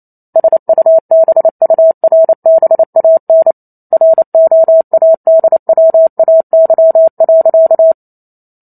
Morse code, heard as a keyed tone, about 28 wpm, 650 Hz. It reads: SUBURBAN ROADWAY.